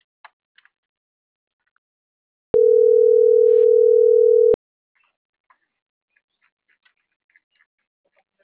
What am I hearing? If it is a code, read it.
Ringback tone (North America), 440 + 480 Hz.